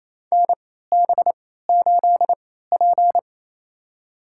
Morse code, heard as a keyed tone, about 28 wpm, spaced as for 16 wpm, 700 Hz. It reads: NB8P
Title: USA Call Signs - 4 Characters - 3x Character Spacing - 28wpm